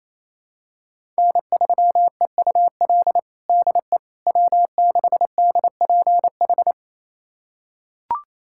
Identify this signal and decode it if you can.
Morse code, heard as a keyed tone, about 28 wpm, 700 Hz. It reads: N3EUL DE W6DPH
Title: USA Callsign DE Callsign - 5 Characters - 28wpm